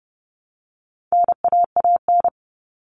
Morse code, heard as a keyed tone, about 30 wpm, 700 Hz. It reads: NAAN
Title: All Cut Numbers - 4-Digit 30wpm